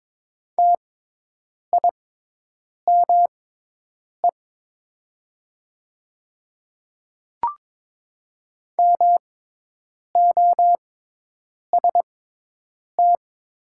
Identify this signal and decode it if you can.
Morse code, heard as a keyed tone, about 22 wpm, spaced as for 8 wpm, 700 Hz. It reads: TIME MOST
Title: 6x Character Spacing - Top 100 Words - 22wpm